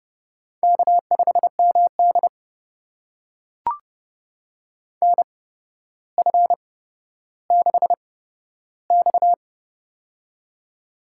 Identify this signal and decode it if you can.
Morse code, heard as a keyed tone, about 30 wpm, spaced as for 13 wpm, 700 Hz. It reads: K5MD NF6X
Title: USA Call Signs - 4 Characters - 8x Character Spacing - 30wpm